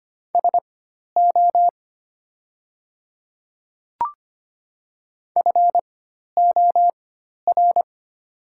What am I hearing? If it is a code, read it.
Morse code, heard as a keyed tone, about 25 wpm, spaced as for 12 wpm, 700 Hz. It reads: SO FOR
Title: 4x Character Spacing - Top 100 Words - 25wpm